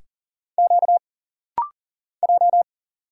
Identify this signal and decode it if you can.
Morse code, heard as a keyed tone, about 40 wpm, 700 Hz. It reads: Q J
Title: Single Letter-Number - Rapid-Fire 40wpm